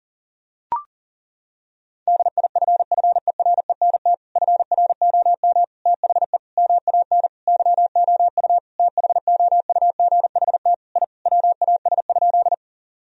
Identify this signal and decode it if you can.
Morse code, heard as a keyed tone, about 40 wpm, 700 Hz. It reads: DIFFERENT FROM THE MAN YOU THOUGHT I WAS?
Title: Sentences from Top 1000 Words - Review Part 3 - 40wpm